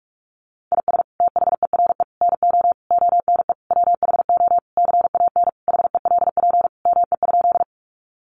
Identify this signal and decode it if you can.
Morse code, heard as a keyed tone, about 44 wpm, 700 Hz. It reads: IS THERE NO ONE WHO CAN HELP ME?
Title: Sentences from Top 200 Words 45wpm